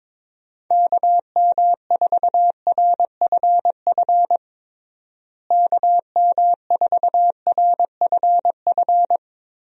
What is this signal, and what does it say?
Morse code, heard as a keyed tone, about 21 wpm, 700 Hz. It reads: KM4RFF KM4RFF
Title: USA Call Signs - 6 Characters - Repeated 3x - 22wpm